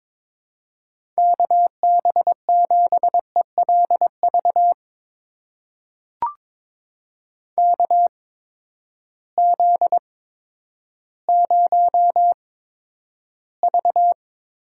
Morse code, heard as a keyed tone, about 21 wpm, spaced as for 11 wpm, 700 Hz. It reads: KB7ELV KZ0V
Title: USA Call Signs - 8x Character Spacing - 22wpm